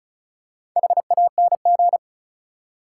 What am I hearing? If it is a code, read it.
Morse code, heard as a keyed tone, about 36 wpm, 700 Hz. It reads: HANG
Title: Four Letter Words - 35wpm